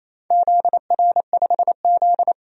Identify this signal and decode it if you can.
Morse code, heard as a keyed tone, about 28 wpm, 700 Hz. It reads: ZR5Z
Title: DXCC-100 Call Signs - 4 Characters - 28wpm